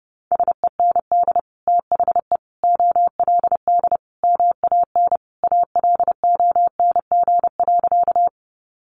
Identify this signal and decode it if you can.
Morse code, heard as a keyed tone, about 30 wpm, 700 Hz. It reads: SEND THE OLD MAN ALONG.
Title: Sentences from Top 1500 Words 30wpm